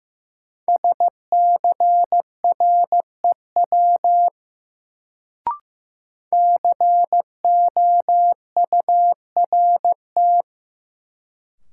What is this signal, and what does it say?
Morse code, heard as a keyed tone, about 15 wpm, 700 Hz. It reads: SCREW COURT
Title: Sentences from Top 1600 Words - New Word Drill Alt - 15wpm